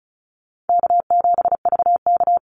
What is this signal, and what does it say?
Morse code, heard as a keyed tone, about 34 wpm, 700 Hz. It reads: K7VK
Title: NA SOTA Chaser Callsigns 35wpm